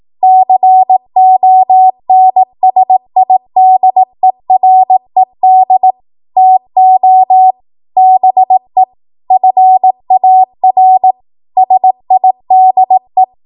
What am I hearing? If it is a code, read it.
Morse code, heard as a keyed tone, about 18 wpm, 750 Hz. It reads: CONSIDERED TO BE FAR SIDE
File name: Morse / 18WPM